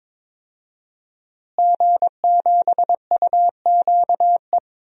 Morse code, heard as a keyed tone, about 22 wpm, 700 Hz. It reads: G7UQE
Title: DXCC-100 Call Signs - 5 Characters - Repeated 2x - 22wpm